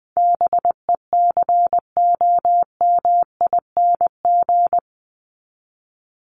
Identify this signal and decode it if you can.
Morse code, heard as a keyed tone, about 20 wpm, 700 Hz. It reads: BECOMING
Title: ING Suffix - Top 100 Words with Base Words 20wpm